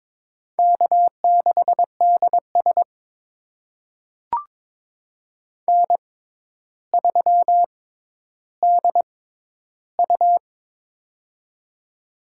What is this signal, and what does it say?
Morse code, heard as a keyed tone, about 22 wpm, spaced as for 11 wpm, 700 Hz. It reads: K6DS N3DU
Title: USA Call Signs - 6x Character Spacing - 22wpm